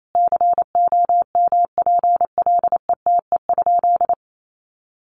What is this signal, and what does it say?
Morse code, heard as a keyed tone, about 28 wpm, 700 Hz. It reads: COMPLETE?
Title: Sentences from Top 1500 Words 28wpm